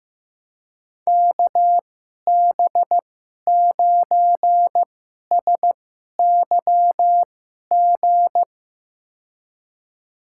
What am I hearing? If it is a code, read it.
Morse code, heard as a keyed tone, about 15 wpm, spaced as for 11 wpm, 700 Hz. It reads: KB9SYG
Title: USA Call Signs - 6 Characters - 2x Character Spacing - 15wpm